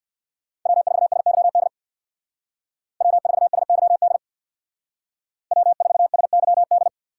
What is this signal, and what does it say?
Morse code, heard as a keyed tone, about 58 wpm, 700 Hz. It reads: W4SYD W4SYD W4SYD
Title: Speed Racing - USA 1x3 Call Signs 50wpm